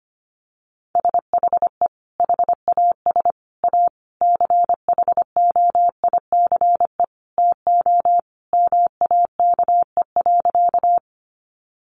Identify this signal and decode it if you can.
Morse code, heard as a keyed tone, about 25 wpm, 700 Hz. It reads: SHE HAS A CHOICE TO MAKE.
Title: Sentences from Top 900 Words 25wpm